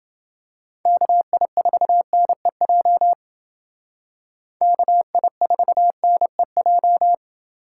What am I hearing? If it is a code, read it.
Morse code, heard as a keyed tone, about 28 wpm, 700 Hz. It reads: KI4NEJ KI4NEJ